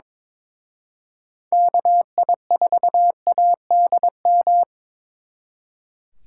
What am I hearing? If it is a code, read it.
Morse code, heard as a keyed tone, about 26 wpm, 700 Hz. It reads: KI4ADM
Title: Speed Racing - USA 2x3 Call Signs 22wpm